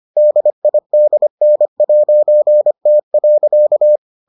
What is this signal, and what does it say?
Morse code, heard as a keyed tone, about 25 wpm, 600 Hz. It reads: DIDN'T.